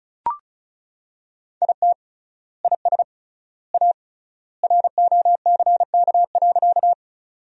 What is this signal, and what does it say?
Morse code, heard as a keyed tone, about 36 wpm, 700 Hz. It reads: IT IS A ROCK.
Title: 3x Word-Spacing - Sentences from Top 400 Words - 35wpm